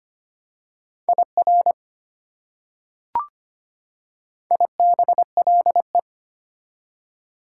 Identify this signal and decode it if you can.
Morse code, heard as a keyed tone, about 25 wpm, 690 Hz. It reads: IR IBLE
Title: Prefixes and Suffixes 25wpm